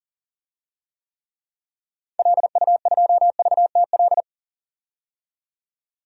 Morse code, heard as a keyed tone, about 40 wpm, 700 Hz. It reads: LU2VTL
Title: DXCC-100 Call Signs - 6 Characters - 40wpm